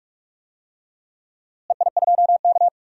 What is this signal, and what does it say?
Morse code, heard as a keyed tone, about 46 wpm, 700 Hz. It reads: EI2K
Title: DXCC-100 Call Signs - 4 Characters - Repeated 3x - 45wpm